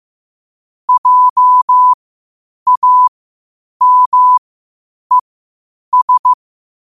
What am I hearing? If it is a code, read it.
Morse code, heard as a keyed tone, about 15 wpm, spaced as for 8 wpm, 1000 Hz. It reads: JAMES
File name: qotdfiles / 8wpm